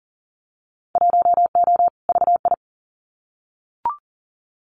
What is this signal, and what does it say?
Morse code, heard as a keyed tone, about 42 wpm, 700 Hz. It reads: 1O VI
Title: Winter Field Day Exchanges Encore 40wpm